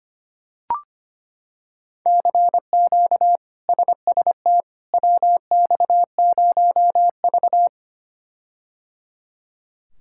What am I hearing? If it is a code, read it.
Morse code, heard as a keyed tone, about 25 wpm, 700 Hz. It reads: CQ SST WX0V
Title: K1USN SST QSO Practice - 25wpm